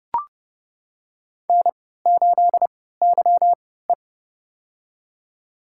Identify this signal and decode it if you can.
Morse code, heard as a keyed tone, about 30 wpm, spaced as for 17 wpm, 700 Hz. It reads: N8YE
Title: USA Call Signs - 4 Characters - 3x Character Spacing - 30wpm